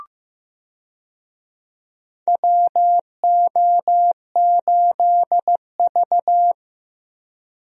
Morse code, heard as a keyed tone, about 15 wpm, 700 Hz. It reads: WO8V
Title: USA Call Signs - 4 Characters - Repeated 2x - 15wpm